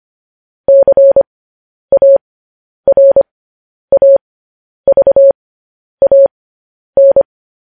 Morse code, heard as a keyed tone, about 25 wpm, spaced as for 10 wpm, 550 Hz. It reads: CARAVAN
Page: news headlines in morse code at 10 wpm